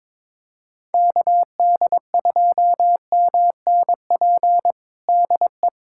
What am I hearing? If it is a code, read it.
Morse code, heard as a keyed tone, about 22 wpm, 700 Hz. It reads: KD2MNP DE
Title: USA Callsign DE Callsign 22wpm